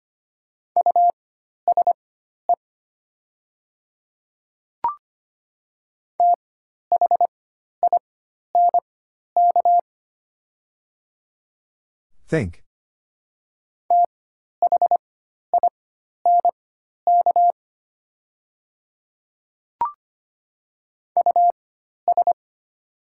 Morse code, heard as a keyed tone, about 25 wpm, spaced as for 12 wpm, 700 Hz. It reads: USE THINK THINK US
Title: 4x Character Spacing - Top 200 Words - 25wpm